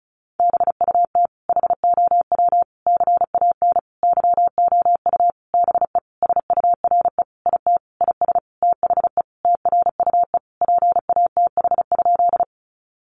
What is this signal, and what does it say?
Morse code, heard as a keyed tone, about 34 wpm, 700 Hz. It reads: BUT HOW CAN YOU BE SURE IT IS THE TRUE PATH?
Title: Sentences from Top 1400 Words 35wpm